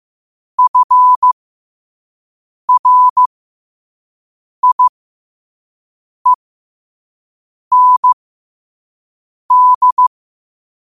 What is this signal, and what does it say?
Morse code, heard as a keyed tone, about 15 wpm, spaced as for 5 wpm, 1000 Hz. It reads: FRIEND
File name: qotdfiles / 5wpm